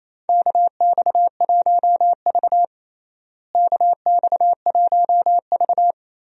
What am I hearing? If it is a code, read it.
Morse code, heard as a keyed tone, about 28 wpm, 700 Hz. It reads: KX1V KX1V
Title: USA Call Signs - 4 Characters - Repeated 3x - 28wpm